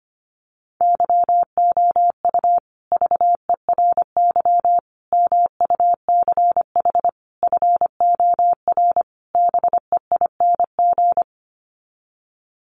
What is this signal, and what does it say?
Morse code, heard as a keyed tone, about 25 wpm, 700 Hz. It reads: YOU VERY MUCH FOR BEING
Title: Sets of 5 Words Encore 25wpm